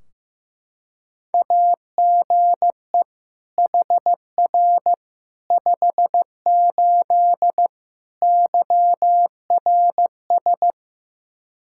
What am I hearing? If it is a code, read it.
Morse code, heard as a keyed tone, about 15 wpm, 700 Hz. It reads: AGE HR 58 YRS
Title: Practice QSO - Exchange 3 - More Info - 15wpm